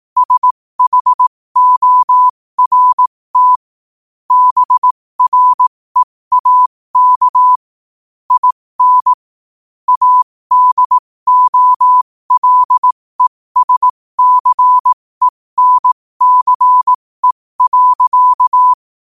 Morse code, heard as a keyed tone, about 18 wpm, spaced as for 15 wpm, 1000 Hz. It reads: SHORT BREAK IN ADOLESCENCE.
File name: qotdfiles / 15wpm